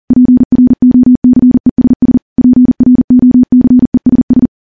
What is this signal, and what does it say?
Morse code, heard as a keyed tone, about 39 wpm, 250 Hz. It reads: PROCESS PROCESS
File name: morse / morse-training-40WPM